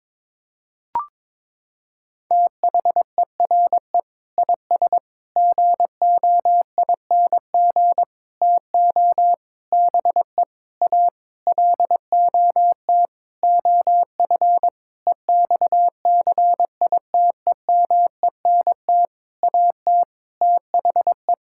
Morse code, heard as a keyed tone, about 22 wpm, 700 Hz. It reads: THERE IS GOING TO BE A LOT OF EXCITEMENT AT THE
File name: Sentences with 7-Word Phrases - 22wpm